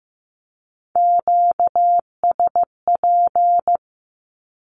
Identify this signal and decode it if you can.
Morse code, heard as a keyed tone, about 15 wpm, 700 Hz. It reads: QSP
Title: Q-Codes with Spoken Meaning - 15wpm